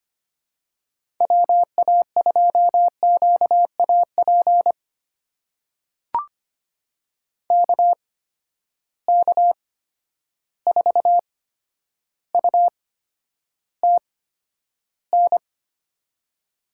Morse code, heard as a keyed tone, about 25 wpm, spaced as for 11 wpm, 700 Hz. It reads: WA2QAP KK4UTN
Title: USA Call Signs - 6 Characters - 8x Character Spacing - 25wpm